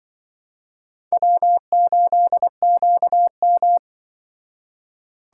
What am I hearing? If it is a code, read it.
Morse code, heard as a keyed tone, about 23 wpm, 700 Hz. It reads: W8QM
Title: Speed Racing - USA Call Signs 20wpm